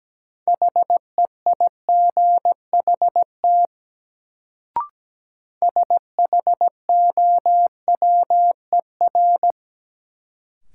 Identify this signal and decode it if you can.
Morse code, heard as a keyed tone, about 17 wpm, 700 Hz. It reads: HEIGHT SHOWER
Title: Sentences from Top 1600 Words - New Word Drill - 17wpm